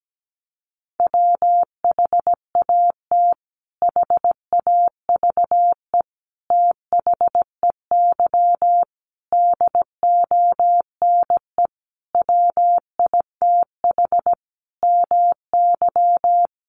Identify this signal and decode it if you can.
Morse code, heard as a keyed tone, about 17 wpm, 700 Hz. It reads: WHAT HAVE THEY DONE WITH MY
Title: Sentences from Top 500 Words 17wpm